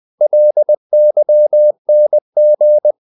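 Morse code, heard as a keyed tone, about 20 wpm, 600 Hz. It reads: LYNG
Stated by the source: Morse code practice of short Danish words at 20 wpm, from CW med Gnister.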